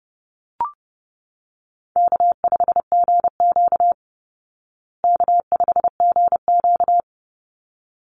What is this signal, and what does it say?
Morse code, heard as a keyed tone, about 28 wpm, 700 Hz. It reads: K5GQ K5GQ
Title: Easy Call Signs Repeated 2x - 30wpm